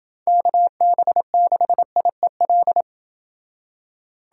Morse code, heard as a keyed tone, about 29 wpm, 700 Hz. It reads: KB6IEL